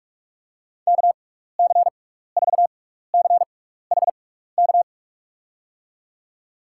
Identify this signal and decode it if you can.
Morse code, heard as a keyed tone, about 47 wpm, spaced as for 16 wpm, 700 Hz. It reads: KC4CHK
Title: USA Call Signs - 6 Characters - 6x Character Spacing - 45wpm